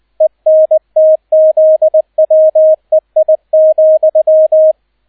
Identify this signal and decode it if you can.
Morse code, heard as a keyed tone, about 20 wpm, 630 Hz, with radio noise. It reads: ENTZWEI,
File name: Erster Streich 20wpm